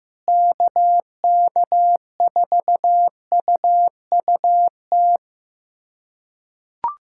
Morse code, heard as a keyed tone, about 15 wpm, 700 Hz. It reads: KK4UUT
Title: USA Callsign DE Callsign - 6 Characters - 15wpm